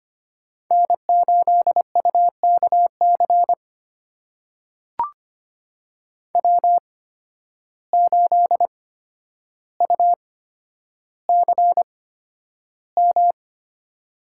Morse code, heard as a keyed tone, about 25 wpm, spaced as for 11 wpm, 700 Hz. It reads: N8UKC W8UCM
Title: USA Call Signs - 5 Characters - 8x Character Spacing - 25wpm